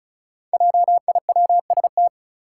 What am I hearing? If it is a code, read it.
Morse code, heard as a keyed tone, about 36 wpm, 700 Hz. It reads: JIWST
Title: Five Challenging Letter-Number Combinations 35wpm